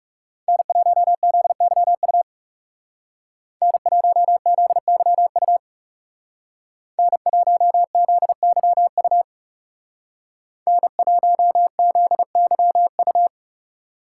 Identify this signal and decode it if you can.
Morse code, heard as a keyed tone, about 39 wpm, 700 Hz. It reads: N1ZYU N1ZYU N1ZYU N1ZYU